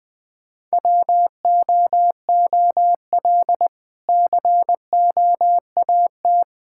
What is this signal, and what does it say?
Morse code, heard as a keyed tone, about 20 wpm, 700 Hz. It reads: WOOL COAT
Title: Difficult Words with Hints - 20wpm